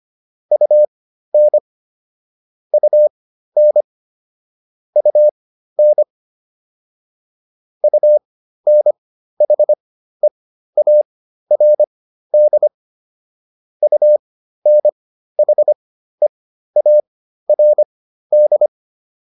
Morse code, heard as a keyed tone, about 25 wpm, spaced as for 13 wpm, 600 Hz. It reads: UN UN UN UNHEARD UNHEARD
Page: intermediate-practice-files